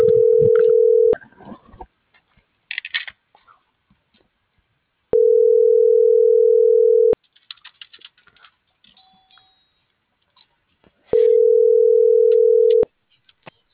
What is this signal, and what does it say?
Ringback tone (North America), 440 + 480 Hz.